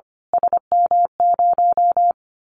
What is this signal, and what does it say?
Morse code, heard as a keyed tone, about 24 wpm, 700 Hz. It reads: SM0
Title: Callsign Prefixes for Top 25 DXCC - Repeated 3x - 25wpm